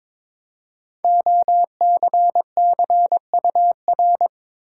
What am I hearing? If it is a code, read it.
Morse code, heard as a keyed tone, about 22 wpm, 700 Hz. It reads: OCCUR
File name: 5 Letter Words - First 3 Letters Given - 22wpm